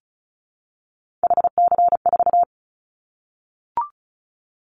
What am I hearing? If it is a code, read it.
Morse code, heard as a keyed tone, about 34 wpm, 700 Hz. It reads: HC4
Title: Callsign Prefixes for Top 100 DXCC - 35wpm